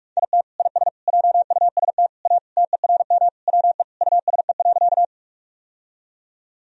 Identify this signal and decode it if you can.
Morse code, heard as a keyed tone, about 46 wpm, 700 Hz. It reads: IT IS JUST A TERM WE USE.
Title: Sentences from Top 1000 Words - Review Part 4 - 45wpm